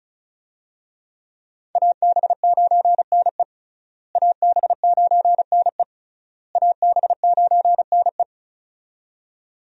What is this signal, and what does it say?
Morse code, heard as a keyed tone, about 36 wpm, 700 Hz. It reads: AB9NE AB9NE AB9NE
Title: USA Call Signs - 5 Characters - Repeated 3x - 35wpm